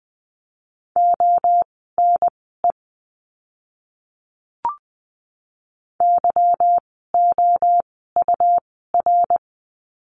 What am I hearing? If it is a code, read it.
Morse code, heard as a keyed tone, about 20 wpm, spaced as for 15 wpm, 700 Hz. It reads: ONE YOUR
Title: 160 Lesson 26f - Letter g - Words 20wpm